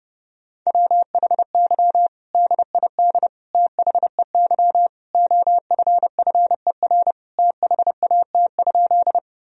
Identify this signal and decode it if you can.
Morse code, heard as a keyed tone, about 32 wpm, 700 Hz. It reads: WHY DID THEY OFFER THAT?
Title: Sentences from Top 1000 Words - Review Part 4 - 30wpm